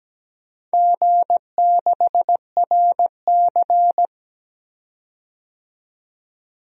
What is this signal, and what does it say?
Morse code, heard as a keyed tone, about 17 wpm, 700 Hz. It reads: G6RC
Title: DXCC-100 Call Signs - 4 Characters - 17wpm